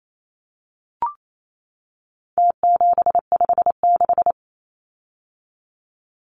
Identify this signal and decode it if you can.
Morse code, heard as a keyed tone, about 28 wpm, 700 Hz. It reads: T756
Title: Numbers - Contest Style - 4-Digit 28wpm